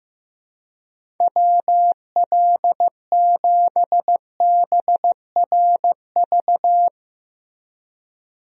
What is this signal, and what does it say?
Morse code, heard as a keyed tone, about 16 wpm, 700 Hz. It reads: WL7BRV
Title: USA Call Signs - 6 Characters - Repeated 2x - 15wpm